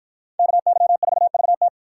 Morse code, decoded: KY4VT

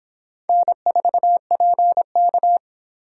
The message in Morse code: N4PK

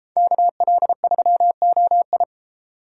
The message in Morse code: KL3OI

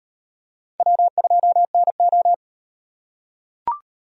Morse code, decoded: W2NO

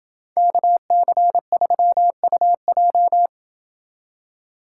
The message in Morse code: KC3UJ

